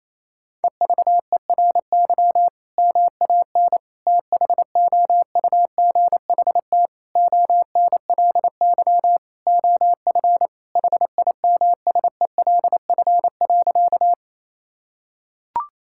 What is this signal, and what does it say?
Morse code, heard as a keyed tone, about 28 wpm, 700 Hz. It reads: EVERY MAN THOUGHT ONLY OF HIMSELF.
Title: Sentences from Top 1600 Words 28wpm